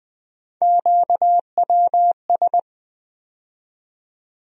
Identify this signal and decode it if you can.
Morse code, heard as a keyed tone, about 20 wpm, 700 Hz. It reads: QWS
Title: USA Call Sign Postfixes - 3 Letters 20wpm